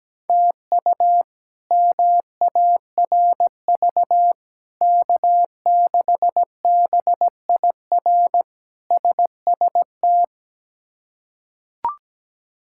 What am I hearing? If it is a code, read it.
Morse code, heard as a keyed tone, about 17 wpm, 700 Hz. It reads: TU MARV K6BIR SST
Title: K1USN SST QSO Practice - 17wpm